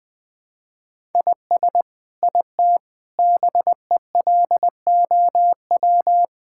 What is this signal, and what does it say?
Morse code, heard as a keyed tone, about 20 wpm, 700 Hz. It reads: IS IT BELOW
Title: Sentences from Top 1000 Words - Review Part 5 - 20wpm